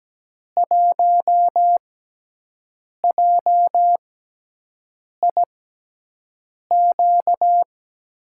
Morse code, heard as a keyed tone, about 17 wpm, spaced as for 6 wpm, 700 Hz. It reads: 1JIQ
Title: USA Call Signs - 6 Characters - 6x Character Spacing - 17wpm